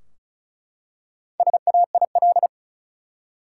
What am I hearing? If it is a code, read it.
Morse code, heard as a keyed tone, about 34 wpm, 700 Hz. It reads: SAIL
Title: Four Letter Words - 35wpm